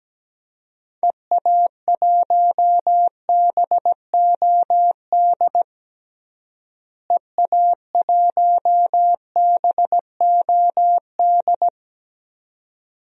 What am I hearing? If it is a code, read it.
Morse code, heard as a keyed tone, about 17 wpm, 700 Hz. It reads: EA1BOD EA1BOD
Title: DXCC-100 Call Signs - 6 Characters - Repeated 2x - 17wpm